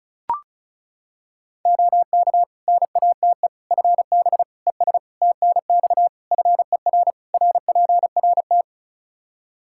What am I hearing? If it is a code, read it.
Morse code, heard as a keyed tone, about 36 wpm, 700 Hz. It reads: OK NATE FB ES TNX FER RPRT
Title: Practice QSO - Exchange 2 - Info - 35wpm